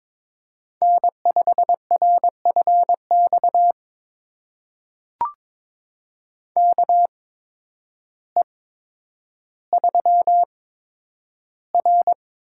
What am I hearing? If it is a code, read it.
Morse code, heard as a keyed tone, about 22 wpm, spaced as for 10 wpm, 700 Hz. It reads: N5RFX KE3R
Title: USA Call Signs - 8x Character Spacing - 22wpm